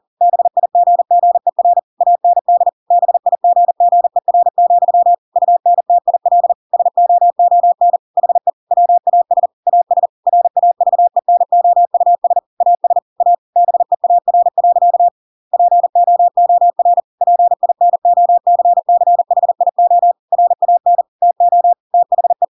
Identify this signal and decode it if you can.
Morse code, heard as a keyed tone, about 40 wpm, 700 Hz. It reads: BIGGER AND BIGGER, UNTIL SOON HE WAS AS RAVENOUS AS A BEAR. POOR PINOCCHIO RAN TO THE
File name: Pinocchio / Pinocchio ch5 40 40 ik1hge